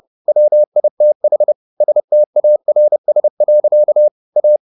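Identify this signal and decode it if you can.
Morse code, heard as a keyed tone, about 30 wpm, 600 Hz. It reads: WITH STARS. A
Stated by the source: Morse code at 30 wpm, the novel Never by Ken Follett in English, from CW med Gnister.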